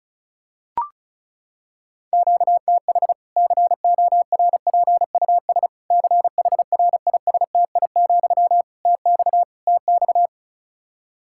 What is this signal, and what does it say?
Morse code, heard as a keyed tone, about 36 wpm, 700 Hz. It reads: QTH CORPUS CHRISTI, TX TX
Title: Practice QSO - Exchange 1 - Beginning - 35wpm